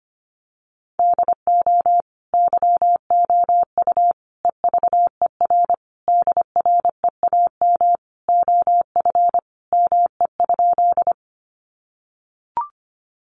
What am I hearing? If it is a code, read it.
Morse code, heard as a keyed tone, about 25 wpm, 700 Hz. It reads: DO YOU EVER DREAM OF ME?
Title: Sentences from Top 1500 Words 25wpm